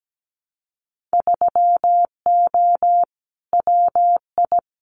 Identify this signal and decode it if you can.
Morse code, heard as a keyed tone, about 17 wpm, 700 Hz. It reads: 3O WI
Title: Winter Field Day Exchanges 17wpm